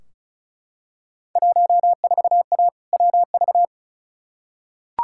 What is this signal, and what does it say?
Morse code, heard as a keyed tone, about 34 wpm, 700 Hz. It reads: 14A WV
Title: Field Day Exchanges Encore - 35wpm